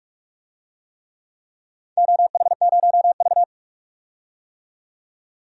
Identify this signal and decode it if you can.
Morse code, heard as a keyed tone, about 46 wpm, 700 Hz. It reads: OH0V
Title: DXCC-100 Call Signs - 4 Characters - 45wpm